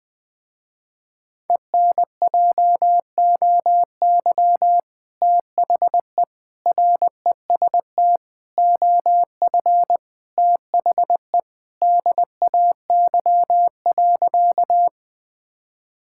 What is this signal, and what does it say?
Morse code, heard as a keyed tone, about 20 wpm, 700 Hz. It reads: ENJOY THE REST OF THE DAY.
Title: Sentences with 2-Word Phrases - Spoken Beginning - 20wpm